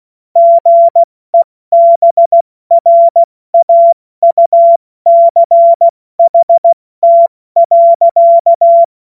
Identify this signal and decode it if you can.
Morse code, heard as a keyed tone, about 16 wpm, spaced as for 14 wpm, 680 Hz. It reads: GEBRAUCHT.
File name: Morsekurs / Witze / Witz 07.November 2023 70 F80